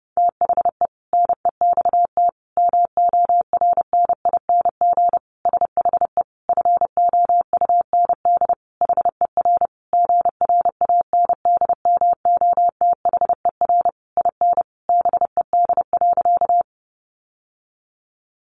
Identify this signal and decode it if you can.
Morse code, heard as a keyed tone, about 30 wpm, 700 Hz. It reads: THE NEXT MORNING SHE FOUND HER GRANDMOTHER IN BED.
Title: Sentences from Top 1200 Words 30wpm